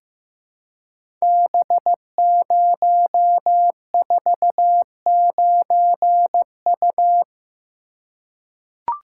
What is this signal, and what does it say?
Morse code, heard as a keyed tone, about 15 wpm, 700 Hz. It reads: B049U
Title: Code Groups - 5 Letter:Numbers - 15wpm